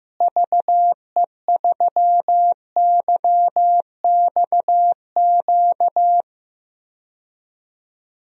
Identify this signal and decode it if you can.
Morse code, heard as a keyed tone, about 15 wpm, 700 Hz. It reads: VE3YXQ
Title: DXCC-100 Call Signs - 6 Characters - Repeated 3x - 15wpm